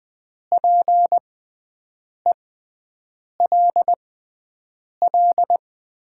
Morse code, heard as a keyed tone, about 20 wpm, spaced as for 7 wpm, 700 Hz. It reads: PELL
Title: 6x Character Spacing - Top 200 Words - 20wpm